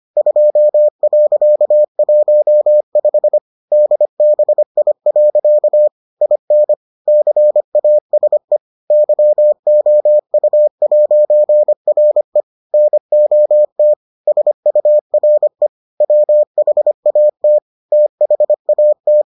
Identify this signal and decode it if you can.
Morse code, heard as a keyed tone, about 25 wpm, 600 Hz. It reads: 2.15 DBI. IN CASE YOU'RE NOT SURE WHAT THAT